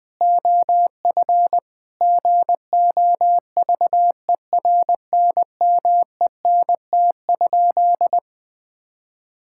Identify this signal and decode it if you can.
Morse code, heard as a keyed tone, about 20 wpm, 700 Hz. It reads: OF GOVERNMENT?
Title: Sentences from Top 1000 Words - Review Part 6 - 20wpm